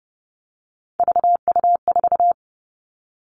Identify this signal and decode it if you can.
Morse code, heard as a keyed tone, about 30 wpm, 700 Hz. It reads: VU4
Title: All Cut Numbers - 3-Digit 30wpm